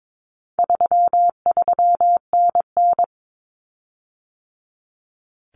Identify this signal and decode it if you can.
Morse code, heard as a keyed tone, about 22 wpm, 700 Hz. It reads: 33NN
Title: Common Cut Numbers - 4-Digit 22wpm